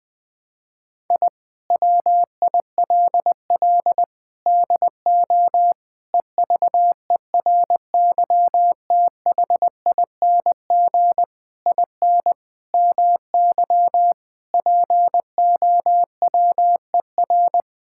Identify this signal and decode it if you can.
Morse code, heard as a keyed tone, about 20 wpm, 700 Hz. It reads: I WILL DO EVERYTHING IN MY POWER
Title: Sentences with 8-Word Phrases - 20wpm